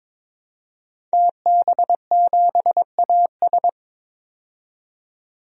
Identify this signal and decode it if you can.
Morse code, heard as a keyed tone, about 22 wpm, 700 Hz. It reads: TB7AS
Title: DXCC-100 Call Signs - 5 Characters - 22wpm